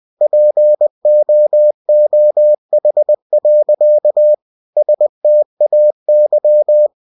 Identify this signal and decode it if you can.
Morse code, heard as a keyed tone, about 20 wpm, 600 Hz. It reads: POOH. STAY
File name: cw-books / winnie-the-pooh / winnie-20wpm-page-0048